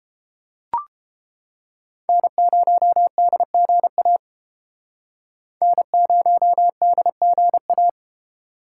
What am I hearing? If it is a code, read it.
Morse code, heard as a keyed tone, about 30 wpm, 700 Hz. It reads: N0DGA N0DGA